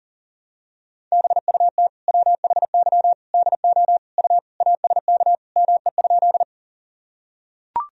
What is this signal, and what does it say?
Morse code, heard as a keyed tone, about 40 wpm, 700 Hz. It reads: BUT WHY DO U ASK ME?
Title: Sentences from Top 200 Words and Abbreviations 40wpm